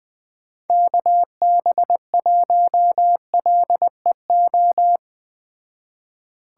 Morse code, heard as a keyed tone, about 20 wpm, 700 Hz. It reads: KB1LEO